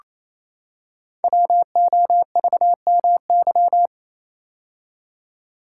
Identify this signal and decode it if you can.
Morse code, heard as a keyed tone, about 27 wpm, 700 Hz. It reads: WOVMY
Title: Code Groups - 5 Letters - 28wpm